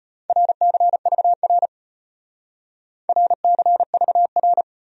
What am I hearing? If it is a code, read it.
Morse code, heard as a keyed tone, about 32 wpm, 700 Hz. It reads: RCVR RCVR